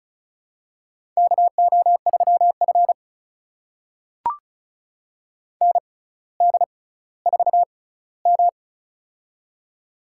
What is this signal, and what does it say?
Morse code, heard as a keyed tone, about 34 wpm, spaced as for 18 wpm, 700 Hz. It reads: KO3F ND4M
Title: USA Call Signs - 4 Characters - 6x Character Spacing - 35wpm